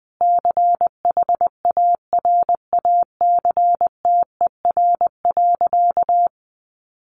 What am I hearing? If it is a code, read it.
Morse code, heard as a keyed tone, about 20 wpm, 700 Hz. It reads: CHARACTER.
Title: Sentences from Top 1200 Words 20wpm